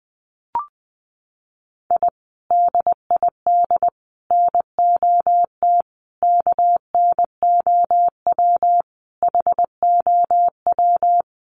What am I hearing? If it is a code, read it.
Morse code, heard as a keyed tone, about 20 wpm, 700 Hz. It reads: I DID NOT KNOW HOW
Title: Sets of 5 Words Encore 20wpm